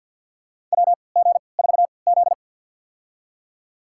Morse code, heard as a keyed tone, about 49 wpm, spaced as for 28 wpm, 700 Hz. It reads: WG4Z
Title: USA Call Signs - 4 Characters - 3x Character Spacing - 50wpm